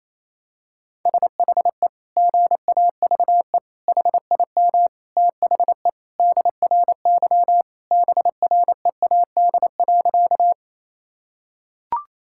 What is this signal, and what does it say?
Morse code, heard as a keyed tone, about 28 wpm, 700 Hz. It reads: SHE GAVE HIM THE DRY BREAD.
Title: Sentences from Top 1600 Words 28wpm